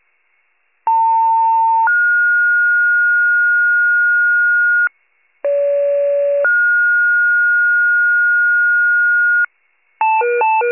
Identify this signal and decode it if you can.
Two-tone page, heard as a one-second tone, then a three-second tone, about 900 then 1430 Hz, about 570 then 1430 Hz. It is heard from 0.9 to 9.4 s.